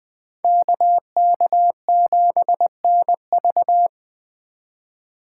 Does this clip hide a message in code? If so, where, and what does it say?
Yes: Morse code from 0.4 to 3.9 s, heard as a keyed tone, about 20 wpm, 700 Hz. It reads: KK7NV